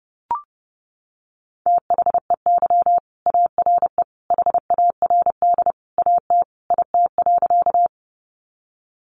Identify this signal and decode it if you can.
Morse code, heard as a keyed tone, about 28 wpm, 700 Hz. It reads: THEY ARE HARD AT IT.
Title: Sentences from Top 300 Words 30wpm